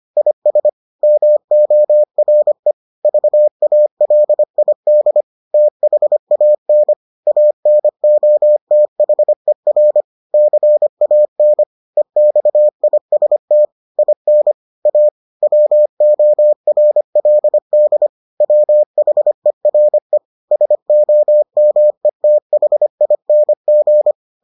Morse code, heard as a keyed tone, about 25 wpm, 600 Hz. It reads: IS MORE VALID THAN ANOTHER CAN EXIST IN A WORLD WHERE SOMETHING